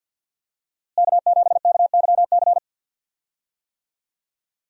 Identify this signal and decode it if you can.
Morse code, heard as a keyed tone, about 50 wpm, 700 Hz. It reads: K7KYC
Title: USA Call Signs - 5 Characters - Repeated 2x - 50wpm